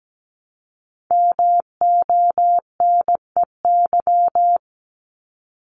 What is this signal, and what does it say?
Morse code, heard as a keyed tone, about 17 wpm, 700 Hz. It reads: MONEY
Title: Multi-letter Phonemes Repeated 3x 17wpm